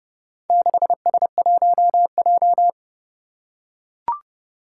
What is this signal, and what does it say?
Morse code, heard as a keyed tone, about 30 wpm, 700 Hz. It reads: 6S1J
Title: Four Challenging Letter-Number Combinations 30wpm